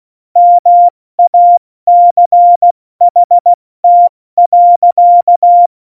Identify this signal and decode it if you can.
Morse code, heard as a keyed tone, about 16 wpm, 700 Hz. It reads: MACHT.
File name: Morsekurs / Witze / Witz 13. Oktober 2020, 70 F80